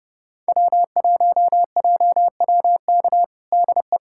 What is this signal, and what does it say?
Morse code, heard as a keyed tone, about 30 wpm, 700 Hz. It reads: W1JWK DE
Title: USA Callsign DE Callsign - 5 Characters - 30wpm